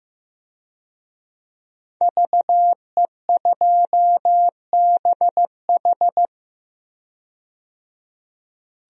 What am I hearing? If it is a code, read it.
Morse code, heard as a keyed tone, about 16 wpm, 700 Hz. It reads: VE2BH
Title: DXCC-100 Call Signs - 5 Characters - 15wpm